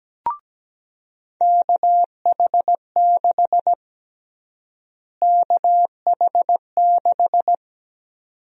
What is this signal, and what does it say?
Morse code, heard as a keyed tone, about 17 wpm, 700 Hz. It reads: KH6 KH6
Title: USA Call Sign Prefixes - Repeated 2x - 17wpm